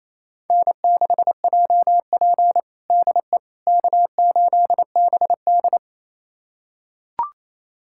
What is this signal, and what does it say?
Morse code, heard as a keyed tone, about 28 wpm, 700 Hz. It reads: N6JP DE K8BD
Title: USA Callsign DE Callsign - 4 Characters - 28wpm